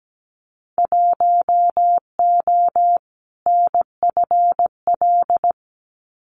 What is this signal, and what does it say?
Morse code, heard as a keyed tone, about 17 wpm, 700 Hz. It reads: 1O NFL
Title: Winter Field Day Exchanges 17wpm